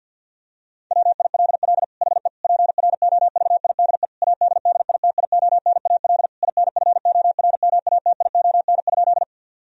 Morse code, heard as a keyed tone, about 50 wpm, 700 Hz. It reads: WILL HE PROVIDE ADDITIONAL INFORMATION?
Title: Sentences from Top 1000 Words - Review Part 4 - 50wpm